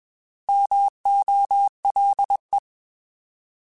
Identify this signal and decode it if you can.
Morse code, heard as a keyed tone, about 21 wpm, 800 Hz. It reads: MOLE
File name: Alphabetical / Alphabetical mp3s / Ms @20